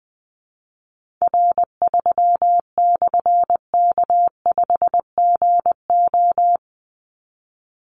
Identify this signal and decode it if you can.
Morse code, heard as a keyed tone, about 20 wpm, 700 Hz. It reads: R3/K5GO
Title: Call Signs - 20wpm